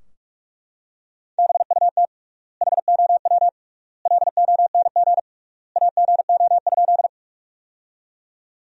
Morse code, heard as a keyed tone, about 42 wpm, 700 Hz. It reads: BUT HOW LONG AGO?
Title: 3x Word-Spacing - Sentences from Top 500 Words - 45wpm